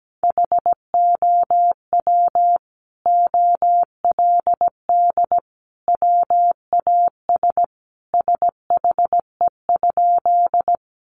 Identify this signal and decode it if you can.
Morse code, heard as a keyed tone, about 17 wpm, 700 Hz. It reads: HOW OLD WAS SHE?
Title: Sentences from Top 200 Words 17wpm